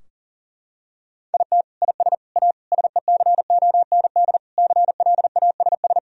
Morse code, heard as a keyed tone, about 37 wpm, 700 Hz. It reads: IT IS A SECOND CLASS
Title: Sentences from Top 1000 Words - Review Part 2 - 40wpm